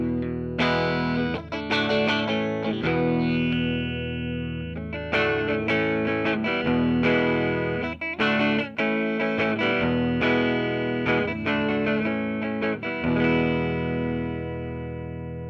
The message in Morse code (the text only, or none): none